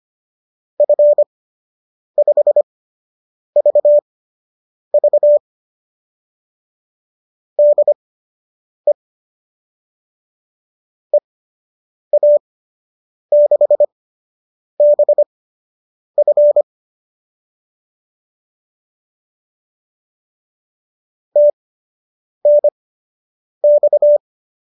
Morse code, heard as a keyed tone, about 25 wpm, spaced as for 8 wpm, 590 Hz. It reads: F5VV DE EA6BF TNX